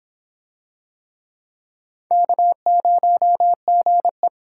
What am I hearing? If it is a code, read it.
Morse code, heard as a keyed tone, about 26 wpm, 700 Hz. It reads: K0GE